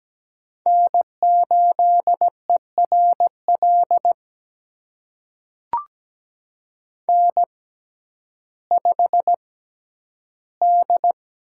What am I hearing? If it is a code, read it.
Morse code, heard as a keyed tone, about 17 wpm, spaced as for 10 wpm, 700 Hz. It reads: N8ERL N5D
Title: USA Call Signs - 5 Characters - 6x Character Spacing - 17wpm